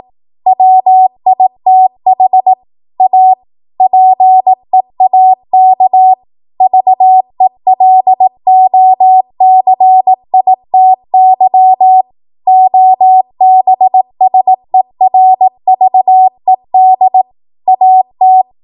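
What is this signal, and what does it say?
Morse code, heard as a keyed tone, about 18 wpm, 750 Hz. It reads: WITH A PEAK VELOCITY OBSERVED AT
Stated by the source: ARRL 18 wpm Morse code practice archive transmission.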